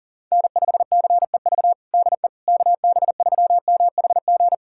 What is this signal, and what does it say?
Morse code, heard as a keyed tone, about 40 wpm, 700 Hz. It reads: N5CEV DE KB3MHG
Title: USA Callsign DE Callsign 40wpm